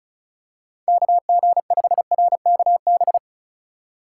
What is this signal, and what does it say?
Morse code, heard as a keyed tone, about 36 wpm, 700 Hz. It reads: KG5RKB
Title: USA Call Signs - 6 Characters - 4x Character Spacing - 35wpm